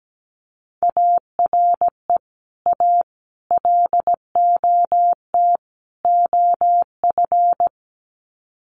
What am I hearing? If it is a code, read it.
Morse code, heard as a keyed tone, about 17 wpm, 700 Hz. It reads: ARE A LOT OF
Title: Sets of 4 Words 17wpm